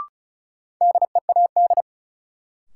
Morse code, heard as a keyed tone, about 36 wpm, 700 Hz. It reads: DEAD
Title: Sentences from Top 1500 Words - New Word Drill - 35wpm